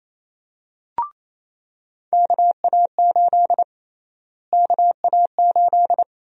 Morse code, heard as a keyed tone, about 28 wpm, 700 Hz. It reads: KA8 KA8
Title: USA Call Sign Prefixes - Repeated 2x - 28wpm